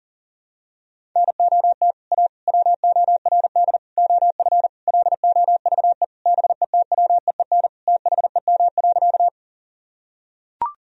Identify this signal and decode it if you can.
Morse code, heard as a keyed tone, about 40 wpm, 700 Hz. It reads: NOT A WORD OF LOVE BETWEEN THEM.
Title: Sentences from Top 1000 Words - Review Part 1 - 40wpm